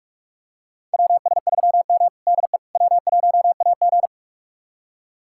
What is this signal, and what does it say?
Morse code, heard as a keyed tone, about 41 wpm, 700 Hz. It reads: WS3M DE W1AG
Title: USA Callsign DE Callsign - 4 Characters - 45wpm